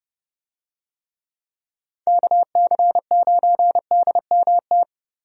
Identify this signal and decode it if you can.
Morse code, heard as a keyed tone, about 30 wpm, 700 Hz. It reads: KC9DMT
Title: USA Call Signs - 6 Characters - Repeated 2x - 30wpm